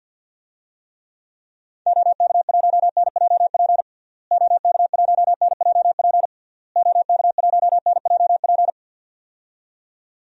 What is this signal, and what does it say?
Morse code, heard as a keyed tone, about 46 wpm, 700 Hz. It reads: OK1NJP OK1NJP OK1NJP
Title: DXCC-100 Call Signs - 6 Characters - Repeated 3x - 50wpm